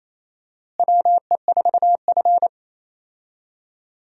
Morse code, heard as a keyed tone, about 28 wpm, 700 Hz. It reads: WE4F
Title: USA Call Signs - 4 Characters - Repeated 2x - 28wpm